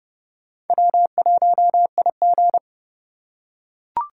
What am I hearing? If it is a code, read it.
Morse code, heard as a keyed tone, about 29 wpm, 700 Hz. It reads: W1IG